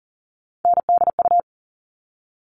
Morse code, heard as a keyed tone, about 40 wpm, 700 Hz. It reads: NDU
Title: All Cut Numbers - 3-Digit 40wpm